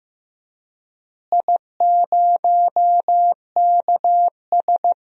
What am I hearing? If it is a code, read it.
Morse code, heard as a keyed tone, about 16 wpm, 700 Hz. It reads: I0KS
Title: DXCC-100 Call Signs - 4 Characters - Repeated 2x - 15wpm